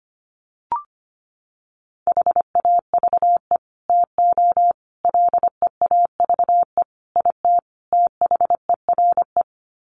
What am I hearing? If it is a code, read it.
Morse code, heard as a keyed tone, about 25 wpm, 700 Hz. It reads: HAVE TO LEAVE IT THERE
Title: Sets of 5 Words Encore2 25wpm